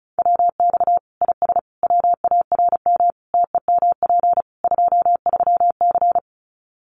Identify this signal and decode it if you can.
Morse code, heard as a keyed tone, about 34 wpm, 700 Hz. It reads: WX IS WARM TEMP 23C
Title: Common QSO Phrases 35wpm